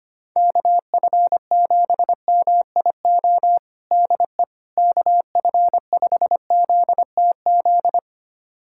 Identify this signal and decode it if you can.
Morse code, heard as a keyed tone, about 25 wpm, 700 Hz. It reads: KF7MIO DE KF5ZTZ
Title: USA Callsign DE Callsign 25wpm